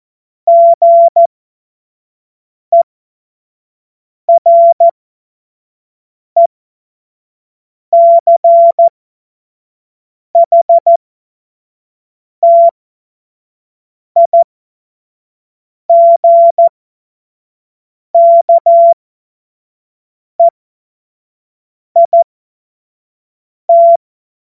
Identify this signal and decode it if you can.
Morse code, heard as a keyed tone, about 14 wpm, spaced as for 5 wpm, 680 Hz. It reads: GERECHTIGKEIT
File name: Morsekurs / Witze / Witz 21. Juni 2022 25 F70